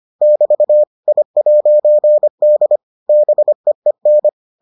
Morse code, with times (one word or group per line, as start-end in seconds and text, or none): 0.2-0.8 <BT>
1.1-2.8 I'D
3.1-4.3 BEEN